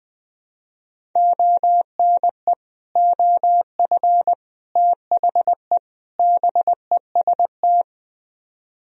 Morse code, heard as a keyed tone, about 20 wpm, 700 Hz. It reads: ONE OF THE BEST
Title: Speed-Racing - Sets of 4 Words 20wpm